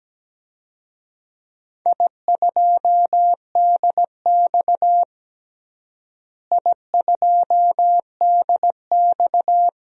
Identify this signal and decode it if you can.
Morse code, heard as a keyed tone, about 17 wpm, 700 Hz. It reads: I2DX I2DX
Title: DXCC-100 Call Signs - 4 Characters - Repeated 2x - 17wpm